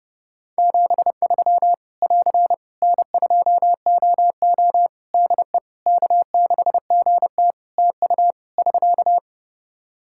Morse code, heard as a keyed tone, about 28 wpm, 700 Hz. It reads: 73 <AR> N2OO DE K6GT TU <SK>
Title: Practice QSO - Exchange 4 - Ending - 30wpm